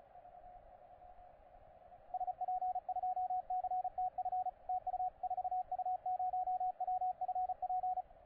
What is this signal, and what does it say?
Morse code, heard as a keyed tone, about 34 wpm, 750 Hz, with radio noise. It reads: SP2CTF TU 4U0WFP
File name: posts / 4u0wfp-15cw